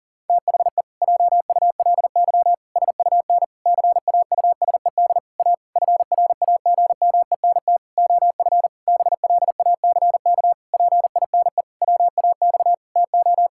Morse code, heard as a keyed tone, about 40 wpm, 700 Hz. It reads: THE JULY SUN CAUSED A FRAGMENT OF BLACK PINE WAX TO
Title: Pangram Challenge v2 - 40wpm